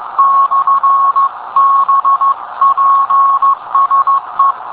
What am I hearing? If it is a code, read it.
Morse code, heard as a keyed tone, about 15 wpm, 1100 Hz. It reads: /BPSE